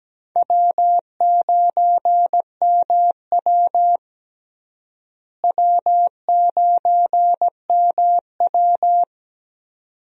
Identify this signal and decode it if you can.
Morse code, heard as a keyed tone, about 17 wpm, 700 Hz. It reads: W9MW W9MW
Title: USA Call Signs - 4 Characters - Repeated 2x - 17wpm